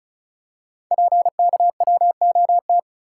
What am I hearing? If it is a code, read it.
Morse code, heard as a keyed tone, about 36 wpm, 700 Hz. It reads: PKWOT
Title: Code Groups - 5 Letters - 35wpm